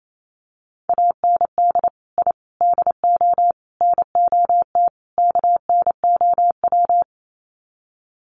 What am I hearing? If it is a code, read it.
Morse code, heard as a keyed tone, about 28 wpm, 700 Hz. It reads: AND I DO NOT KNOW
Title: Sets of 5 Words 28wpm